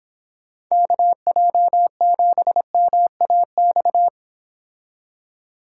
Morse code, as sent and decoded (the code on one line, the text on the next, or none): -.- .--- --... -- .- -..-
KJ7MAX